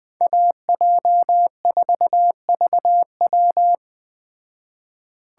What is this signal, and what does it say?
Morse code, heard as a keyed tone, about 20 wpm, 700 Hz. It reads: AJ4VW